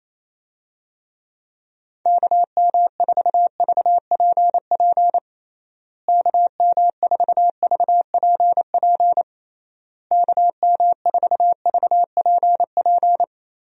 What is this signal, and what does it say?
Morse code, heard as a keyed tone, about 28 wpm, 700 Hz. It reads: KM4VPP KM4VPP KM4VPP
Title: USA Call Signs - 6 Characters - Repeated 3x - 28wpm